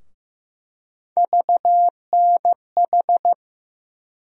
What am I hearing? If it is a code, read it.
Morse code, heard as a keyed tone, about 15 wpm, 700 Hz. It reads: VNH
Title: USA Call Sign Postfixes - 3 Letters 15wpm